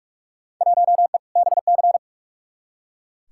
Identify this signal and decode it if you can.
Morse code, heard as a keyed tone, about 41 wpm, 700 Hz. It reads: 1E BC
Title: Field Day Exchanges - 45wpm